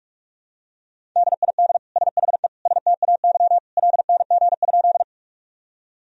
Morse code, heard as a keyed tone, about 48 wpm, 700 Hz. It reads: DID SHE STAY LNG?
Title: Sentences from Top 500 Words and Abbreviations 45wpm